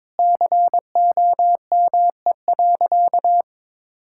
Morse code, heard as a keyed tone, about 21 wpm, 700 Hz. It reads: COME.
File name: practice / 22wpm / Sentences with 2-Word Phrases - 22wpm